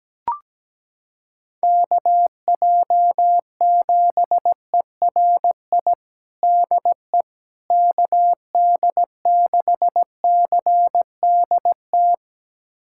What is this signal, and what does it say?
Morse code, heard as a keyed tone, about 17 wpm, 700 Hz. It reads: KJ7ERI DE KD6CDT
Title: USA Callsign DE Callsign 17wpm